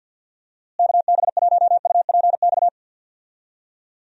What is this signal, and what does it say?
Morse code, heard as a keyed tone, about 46 wpm, 700 Hz. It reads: KB1UPX
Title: USA Call Signs - 6 Characters - Repeated 3x - 50wpm